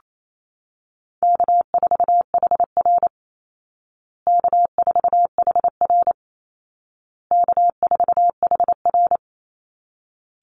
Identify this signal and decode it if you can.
Morse code, heard as a keyed tone, about 28 wpm, 700 Hz. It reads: K4HR K4HR K4HR
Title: Easy Call Signs Repeated 3x - 28wpm